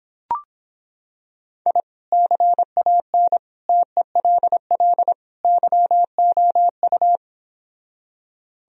Morse code, heard as a keyed tone, about 26 wpm, 700 Hz. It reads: I CAN TELL YOU